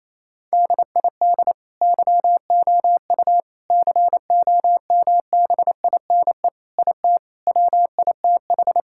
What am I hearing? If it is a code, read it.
Morse code, heard as a keyed tone, about 28 wpm, 700 Hz. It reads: DID YOU COMBINE IT WITH
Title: Sentences from Top 1600 Words 28wpm